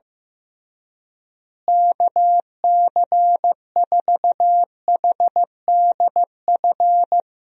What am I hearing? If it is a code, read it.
Morse code, heard as a keyed tone, about 15 wpm, 700 Hz. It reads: KC4HDF
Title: USA Call Signs - 6 Characters - Repeated 3x - 15wpm